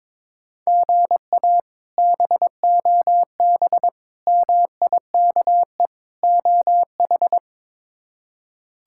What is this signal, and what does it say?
Morse code, heard as a keyed tone, about 21 wpm, 700 Hz. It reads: GA BOB MIKE OH
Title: K1USN SST QSO Practice - 22wpm